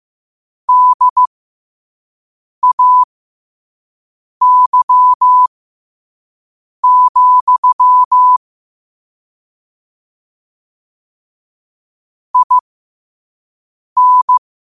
Morse code, heard as a keyed tone, about 15 wpm, spaced as for 5 wpm, 1000 Hz. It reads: DAY, IN
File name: qotdfiles / 5wpm